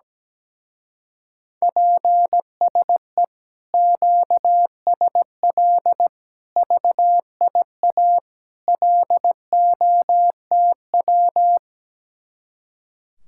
Morse code, heard as a keyed tone, about 17 wpm, 700 Hz. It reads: PSE QSL VIA LOTW
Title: Speed-Racing Common QSO Phrases - 17wpm